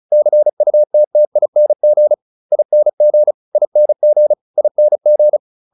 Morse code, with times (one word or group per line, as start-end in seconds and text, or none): 0.1-2.1 CUTTING
2.5-3.3 ING
3.6-4.3 ING
4.6-5.4 ING